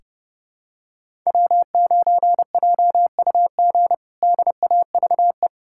Morse code, heard as a keyed tone, about 30 wpm, 700 Hz. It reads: W9JUG DAVE
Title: K1USN SST QSO Practice - 30wpm